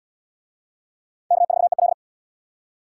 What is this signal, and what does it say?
Morse code, heard as a keyed tone, about 77 wpm, 700 Hz. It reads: K4EL